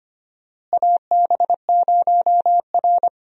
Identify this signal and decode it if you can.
Morse code, heard as a keyed tone, about 25 wpm, 700 Hz. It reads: AB0R
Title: USA Call Signs - 4 Characters - Repeated 2x - 25wpm